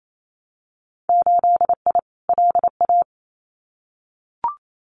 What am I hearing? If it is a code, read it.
Morse code, heard as a keyed tone, about 28 wpm, 700 Hz. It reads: 8I LA
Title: Winter Field Day Exchanges Encore 28wpm